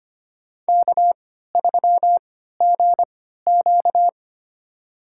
Morse code, heard as a keyed tone, about 25 wpm, spaced as for 14 wpm, 700 Hz. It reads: K3GQ
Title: USA Call Signs - 3x Character Spacing - 25wpm